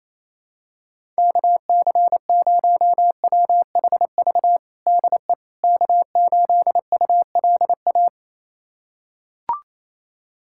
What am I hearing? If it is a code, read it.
Morse code, heard as a keyed tone, about 28 wpm, 700 Hz. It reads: KC0WHV DE K8ULA
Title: USA Callsign DE Callsign 28wpm